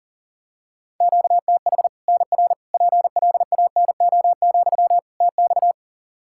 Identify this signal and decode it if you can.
Morse code, heard as a keyed tone, about 40 wpm, 700 Hz. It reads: QTH NR PLANO, TX
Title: Practice QSO - Exchange 1 - Beginning - 40wpm